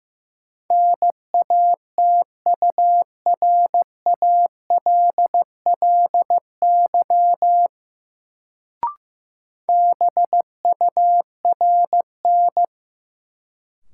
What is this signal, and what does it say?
Morse code, heard as a keyed tone, about 15 wpm, 700 Hz. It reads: NATURALLY BURN
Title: Sentences from Top 1300 Words - New Word Drill - 15wpm